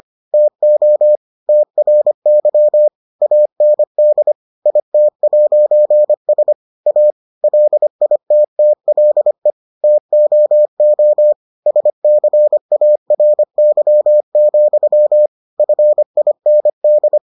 Morse code, heard as a keyed tone, about 25 wpm, 600 Hz. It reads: TO TRY AND IT'S A LITTLE TOO SCARY, FIND